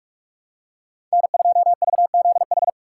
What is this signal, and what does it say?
Morse code, heard as a keyed tone, about 42 wpm, 700 Hz. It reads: N2VZH
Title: USA Call Signs - 5 Characters - 6x Character Spacing - 45wpm